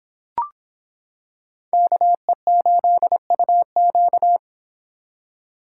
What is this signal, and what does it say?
Morse code, heard as a keyed tone, about 26 wpm, 700 Hz. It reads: KE8UQ